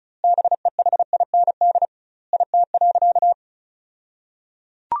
Morse code, heard as a keyed tone, about 34 wpm, 700 Hz. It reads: BEHIND IT.
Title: 2x Word-Spacing - Sentences from Top 500 Words - 35wpm